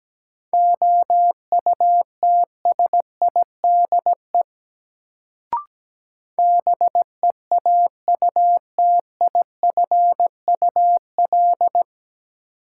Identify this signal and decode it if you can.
Morse code, heard as a keyed tone, about 17 wpm, 700 Hz. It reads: OUTSIDE BEAUTIFUL
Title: Sentences from Top 900 Words - New Word Drill - 17wpm